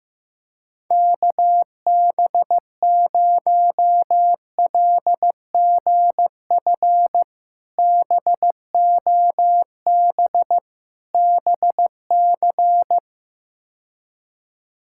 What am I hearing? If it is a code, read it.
Morse code, heard as a keyed tone, about 15 wpm, 700 Hz. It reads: KB0LGF BOB BC
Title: K1USN SST QSO Practice - 15wpm